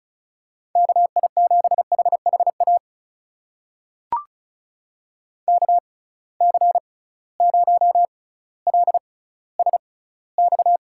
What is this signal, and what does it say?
Morse code, heard as a keyed tone, about 34 wpm, spaced as for 18 wpm, 700 Hz. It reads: KI7HHA KC0LSX
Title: USA Call Signs - 6 Characters - 6x Character Spacing - 35wpm